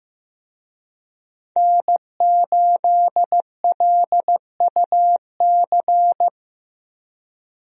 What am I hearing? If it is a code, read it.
Morse code, heard as a keyed tone, about 16 wpm, 700 Hz. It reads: N8LUC
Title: USA Call Signs - 5 Characters - Repeated 2x - 15wpm